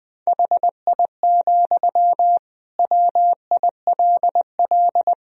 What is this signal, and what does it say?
Morse code, heard as a keyed tone, about 21 wpm, 700 Hz. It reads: HI, WILL
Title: Sentences from Top 1000 Words - Review Part 3 - 20wpm